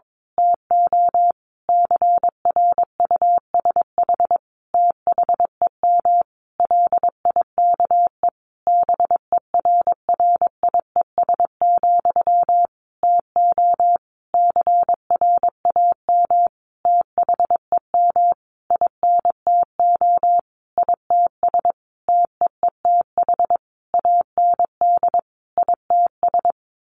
Morse code, heard as a keyed tone, about 22 wpm, 700 Hz. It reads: TO CRUSH THEM LIKE BERRIES, TO CRAM THEM INTO ITS TEETH AND ITS
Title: A Sound Of Thunder 22wpm